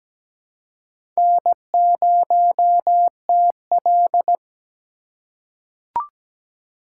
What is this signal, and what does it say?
Morse code, heard as a keyed tone, about 17 wpm, 700 Hz. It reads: N0TL
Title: USA Call Signs - 4 Characters - 2x Character Spacing - 17wpm